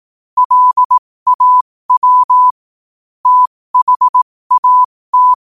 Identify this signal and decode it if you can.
Morse code, heard as a keyed tone, about 18 wpm, spaced as for 15 wpm, 1000 Hz. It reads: LAW THAT
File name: qotdfiles / 15wpm